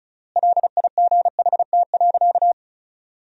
Morse code, heard as a keyed tone, about 34 wpm, 700 Hz. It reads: LIGHT.
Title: 3x Word-Spacing - Sentences from Top 200 Words - 35wpm